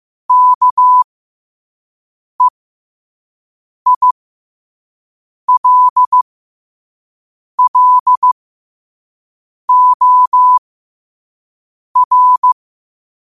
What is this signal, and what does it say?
Morse code, heard as a keyed tone, about 15 wpm, spaced as for 5 wpm, 1000 Hz. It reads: KEILLOR
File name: qotdfiles / 5wpm